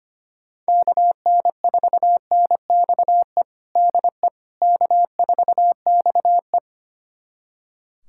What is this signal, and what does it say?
Morse code, heard as a keyed tone, about 25 wpm, 700 Hz. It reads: KN4NXE DE K4XE
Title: USA Callsign DE Callsign 25wpm